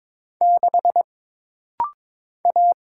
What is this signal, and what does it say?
Morse code, heard as a keyed tone, about 22 wpm, 700 Hz. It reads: 6 A
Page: This rapid-fire timing is geared to push your ICR (Instant Character Recognition) skills.